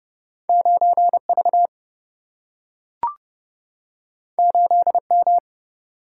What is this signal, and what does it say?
Morse code, heard as a keyed tone, about 31 wpm, 700 Hz. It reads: 9 V 8 M
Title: Single-Letter Single-Number 30wpm